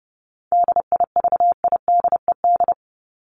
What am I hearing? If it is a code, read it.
Morse code, heard as a keyed tone, about 30 wpm, 700 Hz. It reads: DIVIDED
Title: ED Suffix - Top 100 Words with Base Words 30wpm